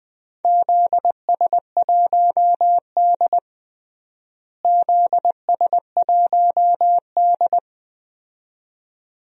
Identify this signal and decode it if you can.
Morse code, heard as a keyed tone, about 20 wpm, 700 Hz. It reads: ZS1D ZS1D
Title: DXCC-100 Call Signs - 4 Characters - Repeated 2x - 20wpm